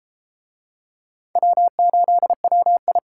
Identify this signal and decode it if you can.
Morse code, heard as a keyed tone, about 31 wpm, 700 Hz. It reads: W8WI